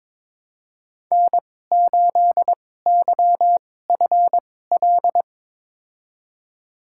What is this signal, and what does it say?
Morse code, heard as a keyed tone, about 22 wpm, spaced as for 16 wpm, 700 Hz. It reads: N8YFL
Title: USA Call Signs - 5 Characters - 2x Character Spacing - 22wpm